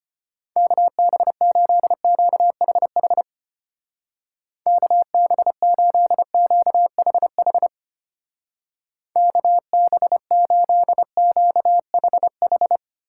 Morse code, heard as a keyed tone, about 33 wpm, 700 Hz. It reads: KB8QHH KB8QHH KB8QHH